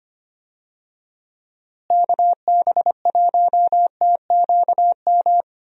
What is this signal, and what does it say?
Morse code, heard as a keyed tone, about 25 wpm, 700 Hz. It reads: KB1TQM
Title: USA Call Signs - 6 Characters - Repeated 3x - 25wpm